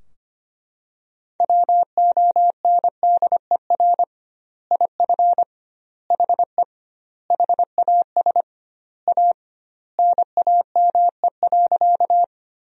Morse code, heard as a keyed tone, about 25 wpm, 700 Hz. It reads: WONDER IF HE HAS A NAME.
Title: 2x Word-Spacing - Sentences from Top 500 Words - 25wpm